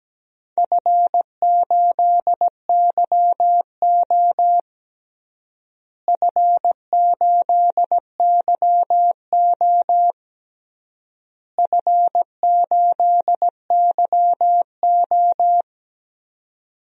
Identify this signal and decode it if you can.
Morse code, heard as a keyed tone, about 17 wpm, 700 Hz. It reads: F8YO F8YO F8YO
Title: DXCC-100 Call Signs - 4 Characters - Repeated 3x - 17wpm